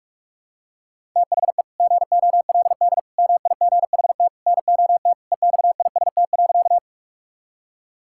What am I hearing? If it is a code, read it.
Morse code, heard as a keyed tone, about 44 wpm, 700 Hz. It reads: THE GOLD MIGHT NOT EXIST.
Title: Sentences from Top 1000 Words - Review Part 5 - 45wpm